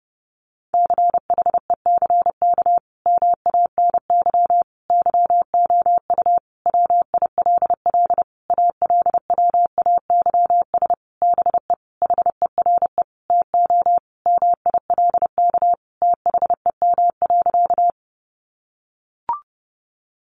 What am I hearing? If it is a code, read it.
Morse code, heard as a keyed tone, about 30 wpm, 700 Hz. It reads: CHECK MANY YOU WILL ALWAYS BE HERE TO MILK THEM.
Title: Sentences from Top 1500 Words 30wpm